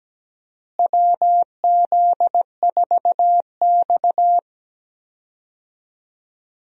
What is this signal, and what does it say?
Morse code, heard as a keyed tone, about 17 wpm, 700 Hz. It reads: WZ4X
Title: USA Call Signs - 4 Characters - Repeated 3x - 17wpm